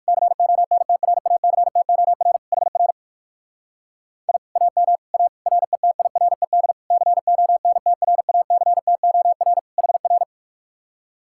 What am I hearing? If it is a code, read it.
Morse code, heard as a keyed tone, about 59 wpm, 700 Hz. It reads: CONTRACTOR HR I AM A RETIRED CONTRACTOR HR